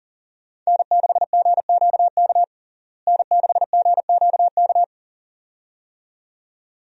Morse code, heard as a keyed tone, about 40 wpm, 700 Hz. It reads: N6GQK N6GQK
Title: USA Call Signs - 5 Characters - Repeated 2x - 40wpm